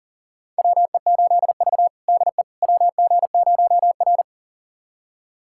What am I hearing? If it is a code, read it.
Morse code, heard as a keyed tone, about 40 wpm, 700 Hz. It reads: WE8V DE WG0R
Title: USA Callsign DE Callsign - 4 Characters - 40wpm